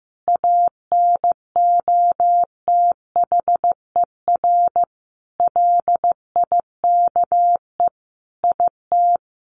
Morse code, heard as a keyed tone, about 15 wpm, 700 Hz. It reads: ANOTHER LIKE IT
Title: Sentences from Top 1400 Words 15wpm